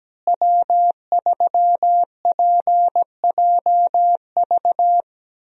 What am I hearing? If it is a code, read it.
Morse code, heard as a keyed tone, about 17 wpm, 700 Hz. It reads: W3PJV
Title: USA Call Signs - 5 Characters - Repeated 3x - 17wpm